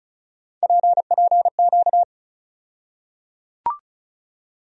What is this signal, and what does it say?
Morse code, heard as a keyed tone, about 34 wpm, 700 Hz. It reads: PPQ
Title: USA Call Sign Postfixes - 3 Letters 35wpm